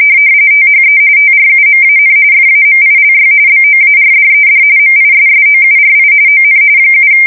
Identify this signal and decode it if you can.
RTTY, about 45.45 baud. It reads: THE TRUMAN FAMILY BOARDED MISSOURI ON 7
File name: fsk170 / USS-Missouri